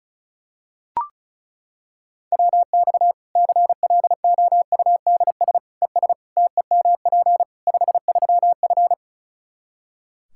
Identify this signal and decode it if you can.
Morse code, heard as a keyed tone, about 36 wpm, 700 Hz. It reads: WX CLOUDS ES TEMP 53F
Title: Practice QSO - Exchange 2 - Info - 35wpm